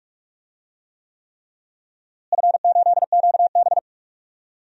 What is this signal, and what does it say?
Morse code, heard as a keyed tone, about 43 wpm, 700 Hz. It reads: F8QB
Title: DXCC-100 Call Signs - 4 Characters - Repeated 3x - 45wpm